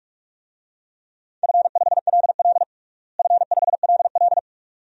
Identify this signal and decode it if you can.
Morse code, heard as a keyed tone, about 41 wpm, 700 Hz. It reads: F5LL F5LL
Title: DXCC-100 Call Signs - 4 Characters - Repeated 2x - 45wpm